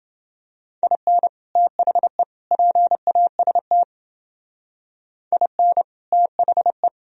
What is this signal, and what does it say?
Morse code, heard as a keyed tone, about 29 wpm, 700 Hz. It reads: IN THE PAST IN THE